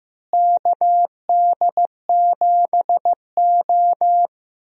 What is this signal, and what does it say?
Morse code, heard as a keyed tone, about 15 wpm, 700 Hz. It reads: KD7O
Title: USA Call Signs - 4 Characters - Repeated 3x - 15wpm